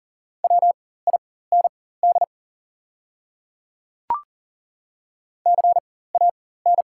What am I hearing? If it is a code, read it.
Morse code, heard as a keyed tone, about 40 wpm, spaced as for 19 wpm, 690 Hz. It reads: WIND CAN
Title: 4x Character Spacing - Top 500 Words - 40wpm